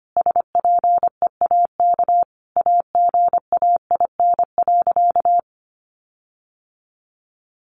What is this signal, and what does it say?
Morse code, heard as a keyed tone, about 25 wpm, 700 Hz. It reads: SPEAK AGAIN.
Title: Sentences from Top 1500 Words 25wpm